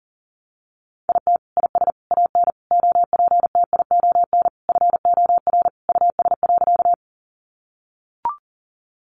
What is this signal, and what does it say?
Morse code, heard as a keyed tone, about 40 wpm, 700 Hz. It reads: IT IS AN OPTION FOR US.
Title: Sentences from Top 1300 Words 40wpm